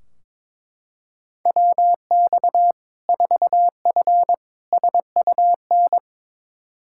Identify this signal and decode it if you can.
Morse code, heard as a keyed tone, about 22 wpm, 700 Hz. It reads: WX 4F SUN
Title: Practice QSO - Exchange 2 - Info - 22wpm